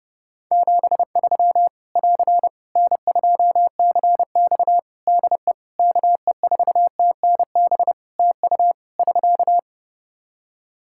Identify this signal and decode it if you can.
Morse code, heard as a keyed tone, about 30 wpm, 700 Hz. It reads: 73 <AR> N2CX DE KE4TNB TU <SK>
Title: Practice QSO - Exchange 4 - Ending - 30wpm